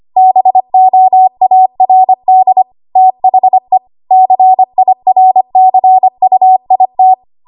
Morse code, heard as a keyed tone, about 25 wpm, 750 Hz. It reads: BOARD THE CIRCUIT
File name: Archive / 25 WPM / 25WPM